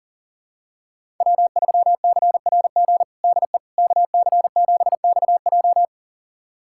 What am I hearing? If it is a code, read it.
Morse code, heard as a keyed tone, about 40 wpm, 700 Hz. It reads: W3CRG DE KC7XJ
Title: USA Callsign DE Callsign - 5 Characters - 40wpm